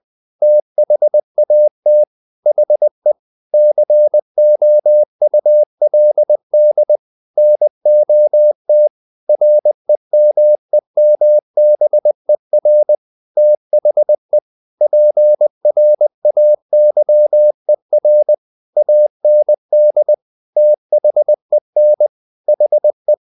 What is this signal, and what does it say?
Morse code, heard as a keyed tone, about 20 wpm, 600 Hz. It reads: THAT HE COULD NOT REMEMBER THE PRAYER AND THEN HE
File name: the old man and the sea / the old man and the sea part 1 20wpm